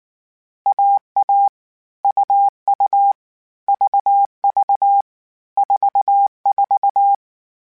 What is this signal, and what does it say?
Morse code, heard as a keyed tone, about 19 wpm, 800 Hz. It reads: AA UU VV 44